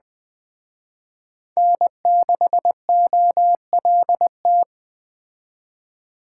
Morse code, heard as a keyed tone, about 20 wpm, 700 Hz. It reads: N6OLT
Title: Speed Racing - USA 1x3 Call Signs 17wpm